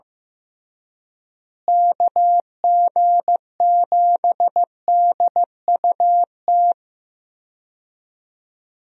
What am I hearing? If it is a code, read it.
Morse code, heard as a keyed tone, about 15 wpm, 700 Hz. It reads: KG7DUT
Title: USA Call Signs - 6 Characters - Repeated 3x - 15wpm